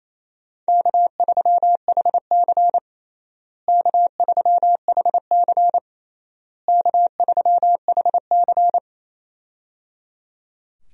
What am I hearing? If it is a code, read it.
Morse code, heard as a keyed tone, about 28 wpm, 700 Hz. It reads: K3HC K3HC K3HC
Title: USA Call Signs - 4 Characters - Repeated 3x - 28wpm